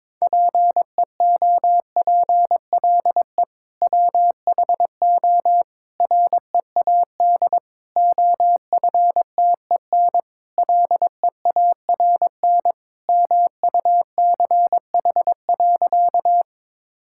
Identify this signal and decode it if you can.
Morse code, heard as a keyed tone, about 22 wpm, 700 Hz. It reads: PEOPLE WHO READ OFTEN LEARN MUCH.
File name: practice / 22wpm / Sentences with 2-Word Phrases - 22wpm